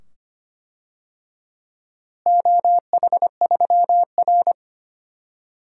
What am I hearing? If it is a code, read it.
Morse code, heard as a keyed tone, about 25 wpm, 700 Hz. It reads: OH3R
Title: DXCC-100 Call Signs - 4 Characters - Repeated 2x - 25wpm